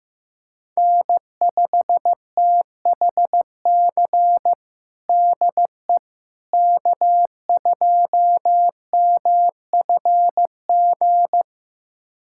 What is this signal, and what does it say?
Morse code, heard as a keyed tone, about 15 wpm, 700 Hz. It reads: N5THC DE K2MFG
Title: USA Callsign DE Callsign - 5 Characters - 15wpm